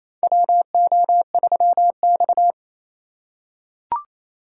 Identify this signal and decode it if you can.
Morse code, heard as a keyed tone, about 28 wpm, 700 Hz. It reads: WO3X
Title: USA Call Signs - 4 Characters - 2x Character Spacing - 28wpm